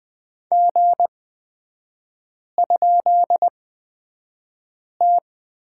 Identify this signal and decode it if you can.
Morse code, heard as a keyed tone, about 20 wpm, spaced as for 5 wpm, 690 Hz. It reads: G?T
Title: 255 Lesson 41e - Period - Three Characters 20wpm